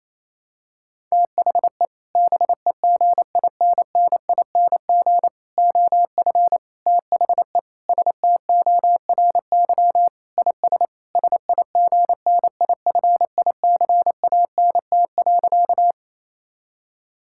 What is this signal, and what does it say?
Morse code, heard as a keyed tone, about 28 wpm, 700 Hz. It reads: THE BEGINNING OF THE STORY IS SIGNIFICANT.
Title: Sentences from Top 1000 Words - Review Part 6 - 28wpm